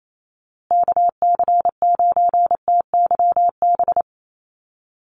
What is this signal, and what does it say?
Morse code, heard as a keyed tone, about 28 wpm, 700 Hz. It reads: KC9TYB
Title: USA Call Signs Repeated 3x 28wpm